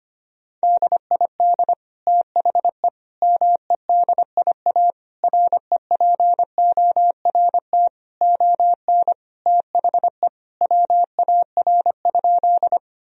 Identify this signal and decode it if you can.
Morse code, heard as a keyed tone, about 25 wpm, 700 Hz. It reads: DID THE MEDIA REPORT ON THE WAR?
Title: Sentences from Top 1000 Words - Review Part 4 - 25wpm